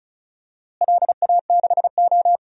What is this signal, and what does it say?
Morse code, heard as a keyed tone, about 34 wpm, 700 Hz. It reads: LA6O
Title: DXCC-100 Call Signs - 4 Characters - Repeated 2x - 35wpm